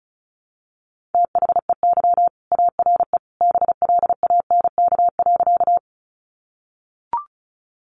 Morse code, heard as a keyed tone, about 34 wpm, 700 Hz. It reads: THEY ARE BLANK.
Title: Sentences from Top 1500 Words 35wpm